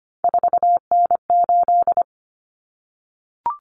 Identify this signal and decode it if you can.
Morse code, heard as a keyed tone, about 25 wpm, 700 Hz. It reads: 4N8
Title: Common Cut Numbers - 3-Digit 25wpm